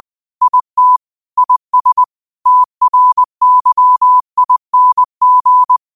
Morse code, heard as a keyed tone, about 20 wpm, 1000 Hz. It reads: IT IS TRYING
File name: qotdfiles / 20wpm